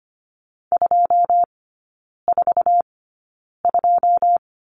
Morse code, heard as a keyed tone, about 25 wpm, spaced as for 9 wpm, 690 Hz. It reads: 242
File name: Lesson 14e - Number 2 - Three Characters 25wpm